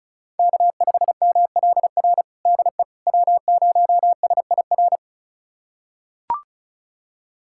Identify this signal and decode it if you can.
Morse code, heard as a keyed tone, about 36 wpm, 700 Hz. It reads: K5MLR DE W0SIR
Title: USA Callsign DE Callsign - 5 Characters - 35wpm